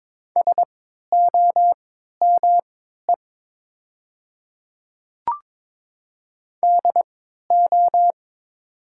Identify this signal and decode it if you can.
Morse code, heard as a keyed tone, about 22 wpm, spaced as for 13 wpm, 700 Hz. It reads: SOME DO
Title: 3x Character Spacing - Top 100 Words - 22wpm